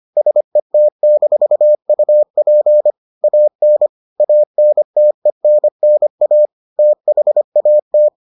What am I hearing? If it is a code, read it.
Morse code, heard as a keyed tone, about 25 wpm, 600 Hz. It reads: SET-UP AN ANTENNA THAT